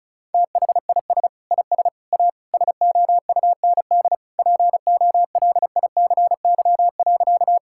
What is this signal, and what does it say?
Morse code, heard as a keyed tone, about 36 wpm, 700 Hz. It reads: THIS IS A SOUND POLICY.
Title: Sentences from Top 1000 Words - Review Part 5 - 35wpm